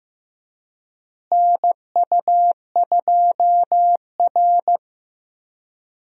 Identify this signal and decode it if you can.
Morse code, heard as a keyed tone, about 15 wpm, 700 Hz. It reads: NU2R
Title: USA Call Signs - 4 Characters - Repeated 3x - 15wpm